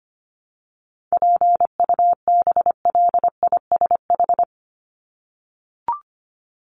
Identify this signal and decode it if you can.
Morse code, heard as a keyed tone, about 25 wpm, 700 Hz. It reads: PUBLISH
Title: ED Suffix - Top 100 Words with Base Words 25wpm